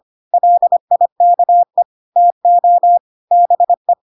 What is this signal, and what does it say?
Morse code, heard as a keyed tone, about 25 wpm, 700 Hz. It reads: LIKE TO BE
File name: Pinocchio / Pinocchio ch4 25 25 ik1hge